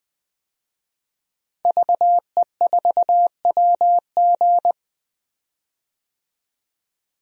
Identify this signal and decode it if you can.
Morse code, heard as a keyed tone, about 20 wpm, 700 Hz. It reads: VE4WG
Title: DXCC-100 Call Signs - 5 Characters - 20wpm